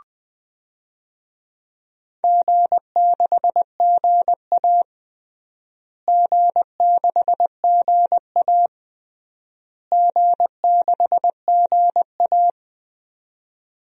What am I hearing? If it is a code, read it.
Morse code, heard as a keyed tone, about 21 wpm, 700 Hz. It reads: G6GA G6GA G6GA
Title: DXCC-100 Call Signs - 4 Characters - Repeated 3x - 20wpm